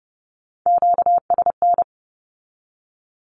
Morse code, heard as a keyed tone, about 30 wpm, 700 Hz. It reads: QSN
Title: Q-Codes with Spoken Meaning - 30wpm